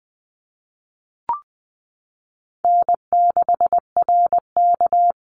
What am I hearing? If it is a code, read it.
Morse code, heard as a keyed tone, about 20 wpm, 700 Hz. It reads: N6RK
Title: Easy Call Signs Repeated 3x - 20wpm